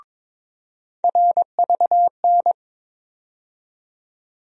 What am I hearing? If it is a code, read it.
Morse code, heard as a keyed tone, about 21 wpm, 700 Hz. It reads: RVN
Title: USA Call Sign Postfixes - 3 Letters 22wpm